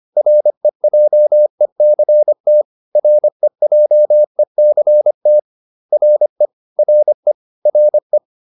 Morse code, heard as a keyed tone, about 25 wpm, 600 Hz. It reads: REJECT REJECT RE RE RE